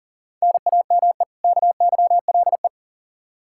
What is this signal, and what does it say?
Morse code, heard as a keyed tone, about 37 wpm, 700 Hz. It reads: NAME KYLE
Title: Practice QSO - Exchange 1 - Beginning - 40wpm